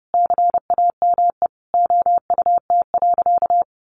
Morse code, heard as a keyed tone, about 30 wpm, 700 Hz. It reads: CAME OUT.
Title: Sentences from Top 1300 Words 30wpm